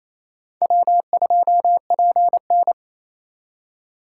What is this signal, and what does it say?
Morse code, heard as a keyed tone, about 35 wpm, 700 Hz. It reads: W2PN
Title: Speed Racing - USA 1x2 Call Signs 28wpm